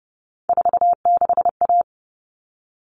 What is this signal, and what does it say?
Morse code, heard as a keyed tone, about 30 wpm, 700 Hz. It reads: 46A
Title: All Cut Numbers - 3-Digit 30wpm